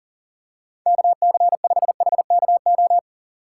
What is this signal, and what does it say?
Morse code, heard as a keyed tone, about 37 wpm, 700 Hz. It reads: KC5HKO
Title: USA Call Signs - 6 Characters - 6x Character Spacing - 40wpm